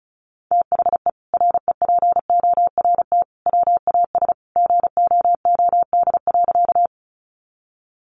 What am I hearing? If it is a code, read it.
Morse code, heard as a keyed tone, about 34 wpm, 700 Hz. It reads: THE REPORT WAS GOOD.
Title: Sentences from Top 900 Words 35wpm